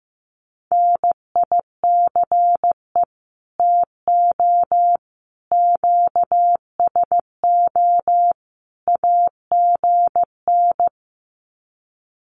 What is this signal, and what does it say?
Morse code, heard as a keyed tone, about 15 wpm, 700 Hz. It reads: NICE TO QSO AGN
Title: Common QSO Phrases 15wpm